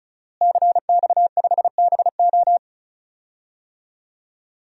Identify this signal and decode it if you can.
Morse code, heard as a keyed tone, about 34 wpm, 700 Hz. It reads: CX5BO
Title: DXCC-100 Call Signs - 5 Characters - Repeated 3x - 35wpm